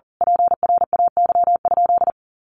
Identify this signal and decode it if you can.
Morse code, heard as a keyed tone, about 38 wpm, 700 Hz. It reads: PRAY?
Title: A Sound Of Thunder 40wpm